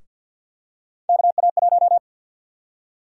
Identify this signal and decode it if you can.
Morse code, heard as a keyed tone, about 50 wpm, 700 Hz. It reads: KA1
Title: USA Call Sign Prefixes - Repeated 2x - 50wpm